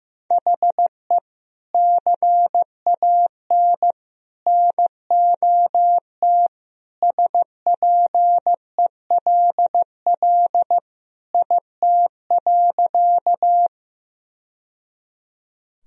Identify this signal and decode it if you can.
Morse code, heard as a keyed tone, about 16 wpm, 700 Hz. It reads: HE CAN NOT SPELL IT.
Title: Sentences from Top 1000 Words - Review Part 1 - 15wpm